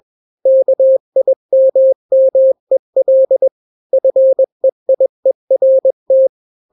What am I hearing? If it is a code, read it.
Morse code, heard as a keyed tone, about 21 wpm, 530 Hz. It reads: KIMMEL FEIERT
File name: daily / 20wpm